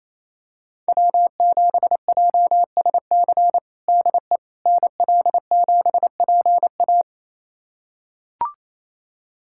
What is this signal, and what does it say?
Morse code, heard as a keyed tone, about 28 wpm, 700 Hz. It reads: W7JSC DE NL7PA
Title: USA Callsign DE Callsign - 5 Characters - 28wpm